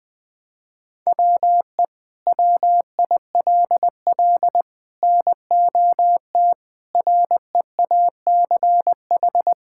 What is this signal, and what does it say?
Morse code, heard as a keyed tone, about 20 wpm, 700 Hz. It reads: WE WILL NOT REACH
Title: Sentences from Top 1000 Words - Review Part 5 - 20wpm